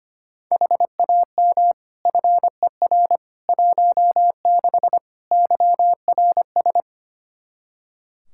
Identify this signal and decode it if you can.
Morse code, heard as a keyed tone, about 25 wpm, 700 Hz. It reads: HAM FER 16 YRS
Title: Practice QSO - Exchange 3 - More Info - 25wpm